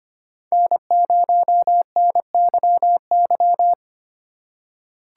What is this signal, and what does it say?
Morse code, heard as a keyed tone, about 25 wpm, 700 Hz. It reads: N0NYY
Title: USA Call Signs - 5 Characters - 2x Character Spacing - 25wpm